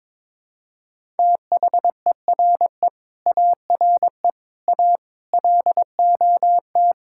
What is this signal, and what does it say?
Morse code, heard as a keyed tone, about 22 wpm, 700 Hz. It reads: THERE ARE A LOT
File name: practice / 22wpm / Sets of 7 Words - 22wpm